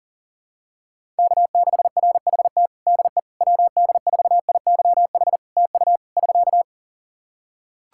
Morse code, heard as a keyed tone, about 38 wpm, 700 Hz. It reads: K6RHT DE WD4IYH TU <SK>
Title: Practice QSO - Exchange 4 - Ending - 40wpm